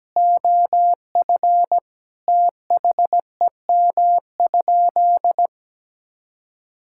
Morse code, heard as a keyed tone, about 17 wpm, 700 Hz. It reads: OF THEM?
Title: Sentences from Top 1000 Words - Review Part 6 - 17wpm